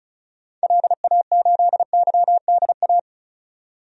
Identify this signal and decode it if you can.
Morse code, heard as a keyed tone, about 36 wpm, 700 Hz. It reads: LA8YDA
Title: DXCC-100 Call Signs - 6 Characters - 35wpm